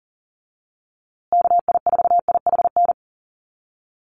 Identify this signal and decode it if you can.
Morse code, heard as a keyed tone, about 40 wpm, 700 Hz. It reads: KI4IHN
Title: USA Call Signs Repeated 3x 40wpm